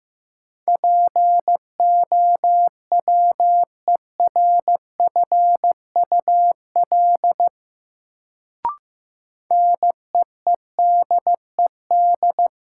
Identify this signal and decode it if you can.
Morse code, heard as a keyed tone, about 15 wpm, 700 Hz. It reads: POWERFUL NEEDED
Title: Sentences from Top 1000 Words - Review Part 6 - New Word Drill - 15wpm